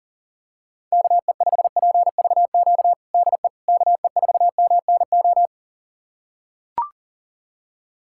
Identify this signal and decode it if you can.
Morse code, heard as a keyed tone, about 40 wpm, 700 Hz. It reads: KE5PVQ DE KE4MNO
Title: USA Callsign DE Callsign 40wpm